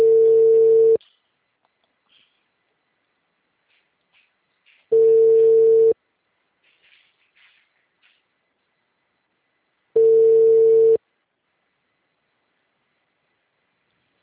Ringback tone (China, 450 Hz).